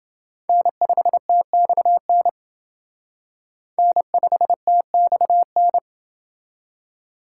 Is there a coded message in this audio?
Yes: Morse code from 0.5 to 5.8 s, heard as a keyed tone, about 29 wpm, 700 Hz. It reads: N5TXN N5TXN